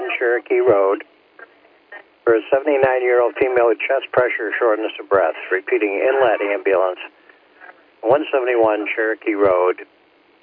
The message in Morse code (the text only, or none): none